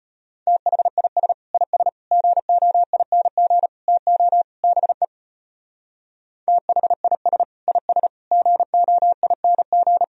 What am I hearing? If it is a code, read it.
Morse code, heard as a keyed tone, about 31 wpm, 700 Hz. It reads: THIS IS GOING TO BE THIS IS GOING
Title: Speed-Racing - Sets of 5 Words 25wpm